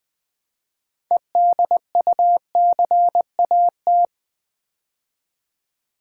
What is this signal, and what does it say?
Morse code, heard as a keyed tone, about 20 wpm, 700 Hz. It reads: EDUCAT
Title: Build A Word - 3 Letters At A Time - 8-Letter Words - 20wpm